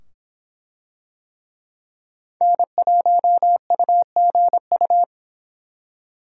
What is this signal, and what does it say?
Morse code, heard as a keyed tone, about 24 wpm, 700 Hz. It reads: N1UGU